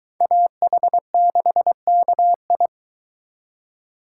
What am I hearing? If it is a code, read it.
Morse code, heard as a keyed tone, about 22 wpm, 700 Hz. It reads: AH6KI